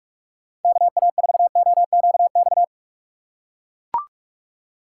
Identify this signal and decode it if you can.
Morse code, heard as a keyed tone, about 41 wpm, 700 Hz. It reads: KA4OQX
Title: USA Call Signs - 6 Characters - 6x Character Spacing - 45wpm